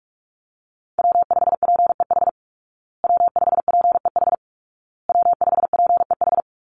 Morse code, heard as a keyed tone, about 44 wpm, 700 Hz. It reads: W5PEH W5PEH W5PEH
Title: Call Signs Repeated 2x - 45wpm